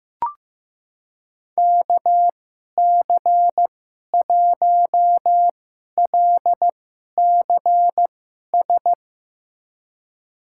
Morse code, heard as a keyed tone, about 15 wpm, spaced as for 11 wpm, 700 Hz. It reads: KC1LCS
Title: USA Call Signs - 6 Characters - 2x Character Spacing - 15wpm